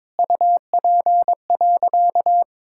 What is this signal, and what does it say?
Morse code, heard as a keyed tone, about 22 wpm, 700 Hz. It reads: UP.
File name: practice / 22wpm / Sentences with 3-Word Phrases - 22wpm